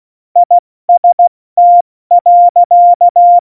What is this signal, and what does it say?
Morse code, heard as a keyed tone, about 16 wpm, spaced as for 14 wpm, 700 Hz. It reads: IST.
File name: Morsekurs / Witze / Witz 13. Oktober 2020, 70 F80